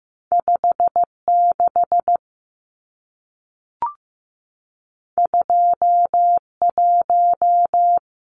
Morse code, heard as a keyed tone, about 15 wpm, 700 Hz. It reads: 5 6 2 1
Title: Numbers - 2-Digit 15wpm